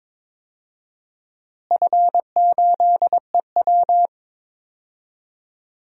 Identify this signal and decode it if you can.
Morse code, heard as a keyed tone, about 22 wpm, 700 Hz. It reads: F8EW
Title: DXCC-100 Call Signs - 4 Characters - Repeated 3x - 22wpm